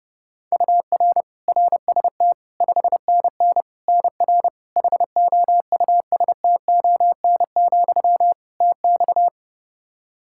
Morse code, heard as a keyed tone, about 32 wpm, 700 Hz. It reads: UR RST 5NN NR HOUSTON, TX
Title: K1USN SST QSO Practice 30wpm